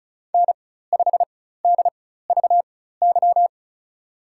Morse code, heard as a keyed tone, about 36 wpm, spaced as for 16 wpm, 700 Hz. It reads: N5DVY
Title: USA Call Signs - 5 Characters - 4x Character Spacing - 35wpm